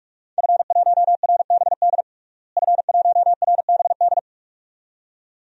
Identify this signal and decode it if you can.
Morse code, heard as a keyed tone, about 42 wpm, 700 Hz. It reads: F1RBD F1RBD
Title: DXCC-100 Call Signs - 5 Characters - Repeated 2x - 45wpm